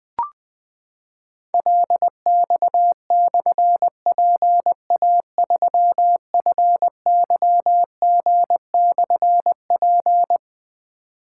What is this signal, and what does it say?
Morse code, heard as a keyed tone, about 20 wpm, 700 Hz. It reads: LX/PA3FYG/P
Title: Difficult Call Signs 20wpm